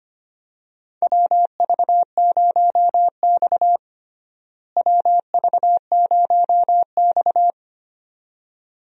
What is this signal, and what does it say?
Morse code, heard as a keyed tone, about 25 wpm, 700 Hz. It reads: WV0X WV0X
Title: USA Call Signs - 4 Characters - Repeated 2x - 25wpm